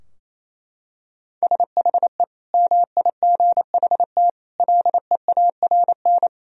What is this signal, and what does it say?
Morse code, heard as a keyed tone, about 28 wpm, 700 Hz. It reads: SHE MIGHT LEARN
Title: Sentences from Top 1000 Words - Review Part 4 - 28wpm